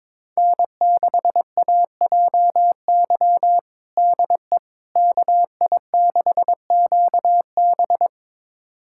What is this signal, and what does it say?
Morse code, heard as a keyed tone, about 22 wpm, 700 Hz. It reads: N6AJY DE KI6QB
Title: USA Callsign DE Callsign - 5 Characters - 22wpm